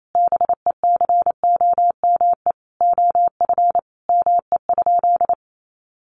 Morse code, heard as a keyed tone, about 28 wpm, 700 Hz. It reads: BECOME OF ME?
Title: Sentences from Top 600 Words 28wpm